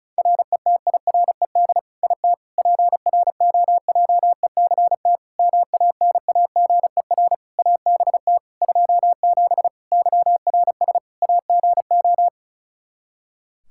Morse code, heard as a keyed tone, about 34 wpm, 700 Hz. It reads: RETIRED IT PROJECT MANAGER ABT 27 YRS AGO
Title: Practice QSO - Exchange 3 - More Info - 35wpm